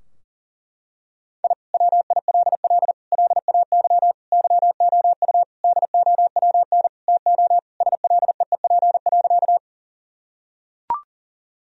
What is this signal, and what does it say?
Morse code, heard as a keyed tone, about 40 wpm, 700 Hz. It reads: I WILL LAY YOU DOWN TO SLEEP.
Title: Sentences from Top 1000 Words - Review Part 2 - 40wpm